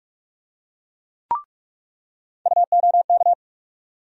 Morse code, heard as a keyed tone, about 46 wpm, 700 Hz. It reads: UOK
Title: USA Call Sign Postfixes - 3 Letters 45wpm